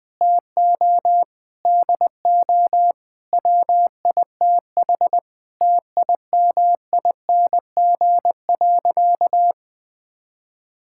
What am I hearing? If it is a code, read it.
Morse code, heard as a keyed tone, about 19 wpm, 700 Hz. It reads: TO DO WITH TIMING.
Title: Sentences with 8-Word Phrases - Spoken Beginning - 20wpm